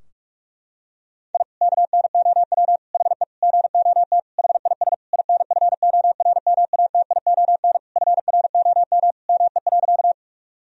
Morse code, heard as a keyed tone, about 41 wpm, 700 Hz. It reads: I KNOW HE GOT HIS INFORMATION FROM ME.
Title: Sentences from Top 1000 Words - Review Part 3 - 45wpm